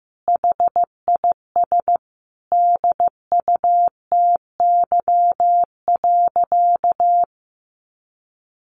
Morse code, heard as a keyed tone, about 15 wpm, 700 Hz. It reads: HIS DUTY.
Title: Sentences from Top 1500 Words 15wpm